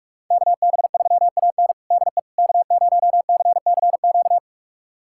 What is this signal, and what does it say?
Morse code, heard as a keyed tone, about 46 wpm, 700 Hz. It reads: KB3AN DE K0CCQ
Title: USA Callsign DE Callsign - 5 Characters - 45wpm